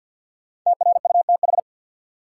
Morse code, heard as a keyed tone, about 53 wpm, 700 Hz. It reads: TRUTH
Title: Sentences from Top 800 Words - New Word Drill - 50wpm